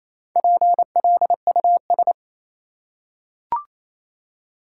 Morse code, heard as a keyed tone, about 28 wpm, 700 Hz. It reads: PLUS
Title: Four Letter Words - 28wpm